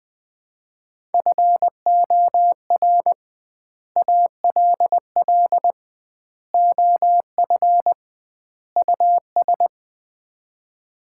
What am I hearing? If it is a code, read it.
Morse code, heard as a keyed tone, about 21 wpm, 700 Hz. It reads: FOR ALL OF US
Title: 2x Word-Spacing - Sets of 4 Words - 20wpm